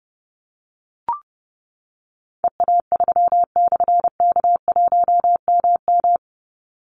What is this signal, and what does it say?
Morse code, heard as a keyed tone, about 30 wpm, 700 Hz. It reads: EA3/K1MM
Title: Call Signs - 30wpm